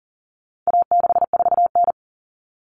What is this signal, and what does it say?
Morse code, heard as a keyed tone, about 40 wpm, 700 Hz. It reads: A64N
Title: Common Cut Numbers - 4-Digit 40wpm